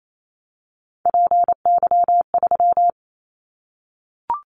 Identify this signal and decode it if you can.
Morse code, heard as a keyed tone, about 28 wpm, 700 Hz. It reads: PY3
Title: Callsign Prefixes for Top 50 DXCC - 28wpm